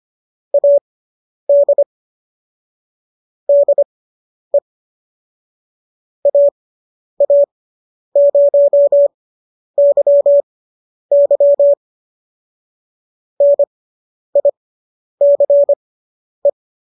Morse code, heard as a keyed tone, about 25 wpm, spaced as for 10 wpm, 570 Hz. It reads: AD DE AA0YY NICE